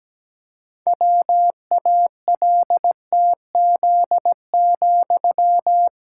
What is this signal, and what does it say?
Morse code, heard as a keyed tone, about 17 wpm, 700 Hz. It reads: WALTZ,
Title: Pangram Challenge v2 - 17wpm